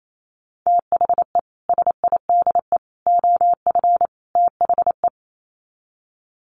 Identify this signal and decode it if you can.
Morse code, heard as a keyed tone, about 28 wpm, 700 Hz. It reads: THE SIDE OF THE
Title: Sets of 4 Words 28wpm